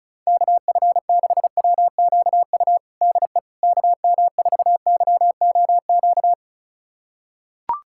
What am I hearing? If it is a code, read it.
Morse code, heard as a keyed tone, about 34 wpm, 700 Hz. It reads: KF6WQU DE KM4YOQ
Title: USA Callsign DE Callsign - 6 Characters - 35wpm